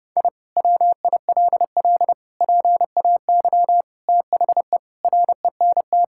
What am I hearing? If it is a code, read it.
Morse code, heard as a keyed tone, about 30 wpm, 700 Hz. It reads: I WILL PAY THE RENT
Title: Sentences from Top 1000 Words - Review Part 5 - 30wpm